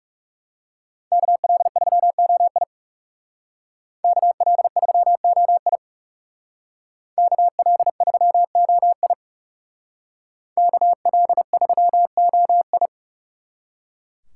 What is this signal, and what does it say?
Morse code, heard as a keyed tone, about 38 wpm, 700 Hz. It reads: KL3OI KL3OI KL3OI KL3OI